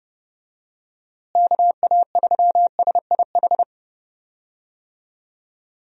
Morse code, heard as a keyed tone, about 30 wpm, 700 Hz. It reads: KA3SIH
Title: USA Call Signs - 6 Characters - Repeated 3x - 30wpm